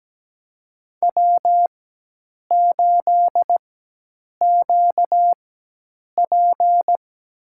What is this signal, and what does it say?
Morse code, heard as a keyed tone, about 17 wpm, spaced as for 8 wpm, 700 Hz. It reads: W8QP
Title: USA Call Signs - 4x Character Spacing - 17wpm